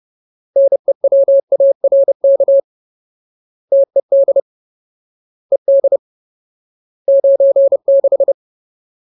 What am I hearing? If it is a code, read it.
Morse code, heard as a keyed tone, about 30 wpm, 550 Hz. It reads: NEWARK TED ED 96
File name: Ultimate Word List / 30WPM Ultimate